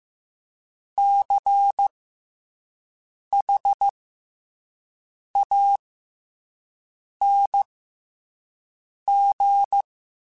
Morse code, heard as a keyed tone, about 15 wpm, spaced as for 5 wpm, 800 Hz. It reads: CHANG
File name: Special INGs-mp3s / Special INGs-mp3s / SlowToFaster ing @10